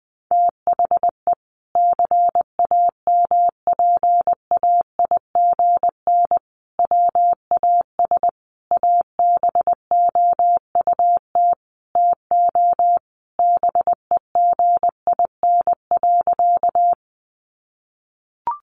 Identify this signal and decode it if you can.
Morse code, heard as a keyed tone, about 20 wpm, 700 Hz. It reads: THE CAMPAIGN WAS ABOUT TO BEGIN.
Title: Sentences from Top 1200 Words 20wpm